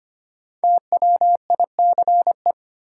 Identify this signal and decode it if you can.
Morse code, heard as a keyed tone, about 25 wpm, 700 Hz. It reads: TWICE
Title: Five Letter Words - 25wpm